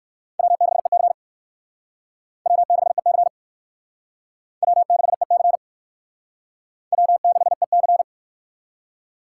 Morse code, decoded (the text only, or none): W6EC W6EC W6EC W6EC